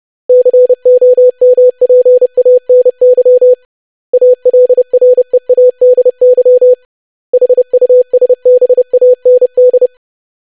Morse code, heard as a keyed tone, about 30 wpm, 500 Hz. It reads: COMPANY ALREADY HUSBAND
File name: lesson / web / basic500-25wpm